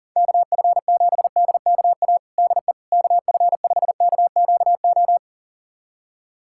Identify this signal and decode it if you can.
Morse code, heard as a keyed tone, about 40 wpm, 700 Hz. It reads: KF7DKA DE KF5KQO
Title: USA Callsign DE Callsign - 6 Characters - 40wpm